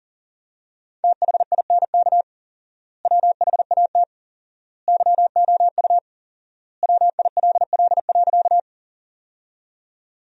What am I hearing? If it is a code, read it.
Morse code, heard as a keyed tone, about 40 wpm, 700 Hz. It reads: THINK WHAT YOU WILL.
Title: 4x Word-Spacing - Sentences from Top 200 Words - 40wpm